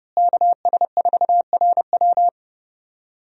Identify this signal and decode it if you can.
Morse code, heard as a keyed tone, about 30 wpm, 700 Hz. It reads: KS4RW
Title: USA Call Signs - 5 Characters - Repeated 3x - 30wpm